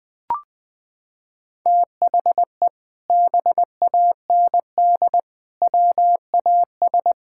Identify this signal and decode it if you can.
Morse code, heard as a keyed tone, about 20 wpm, 700 Hz. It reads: THE BAND WAS
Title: Sentences with 6-Word Phrases - 20wpm